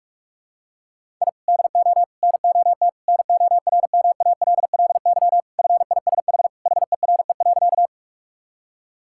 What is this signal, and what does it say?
Morse code, heard as a keyed tone, about 46 wpm, 700 Hz. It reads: I DO NOT NORMALLY FISH HERE.
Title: Sentences from Top 1000 Words - Review Part 6 - 45wpm